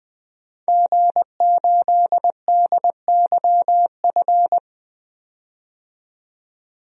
Morse code, heard as a keyed tone, about 21 wpm, 700 Hz. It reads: G8DYF
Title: DXCC-100 Call Signs - 5 Characters - 20wpm